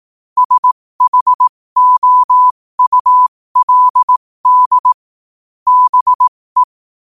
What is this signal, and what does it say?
Morse code, heard as a keyed tone, about 18 wpm, spaced as for 15 wpm, 1000 Hz. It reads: SHOULD BE
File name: qotdfiles / 15wpm